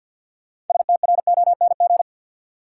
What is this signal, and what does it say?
Morse code, heard as a keyed tone, about 53 wpm, 700 Hz. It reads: STRONG
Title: Sentences from Top 500 Words - New Word Drill - 50wpm